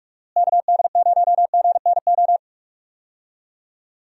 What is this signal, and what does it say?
Morse code, heard as a keyed tone, about 46 wpm, 700 Hz. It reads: KD0GNO
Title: USA Call Signs - 6 Characters - Repeated 2x - 45wpm